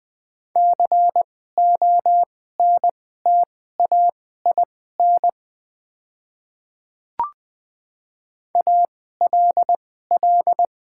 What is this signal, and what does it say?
Morse code, heard as a keyed tone, about 20 wpm, spaced as for 15 wpm, 690 Hz. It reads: CONTAIN ALL
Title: 2x Character Spacing - Top 500 Words - 20wpm